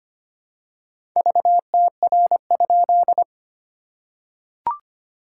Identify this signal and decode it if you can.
Morse code, heard as a keyed tone, about 25 wpm, 700 Hz. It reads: VTR?
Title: SOTA QSO Practice 25wpm